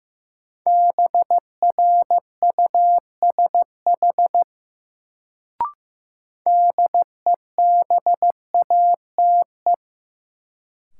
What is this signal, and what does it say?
Morse code, heard as a keyed tone, about 15 wpm, 700 Hz. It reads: BRUSH DEBATE
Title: Sentences from Top 1100 Words - New Word Drill Alt - 15wpm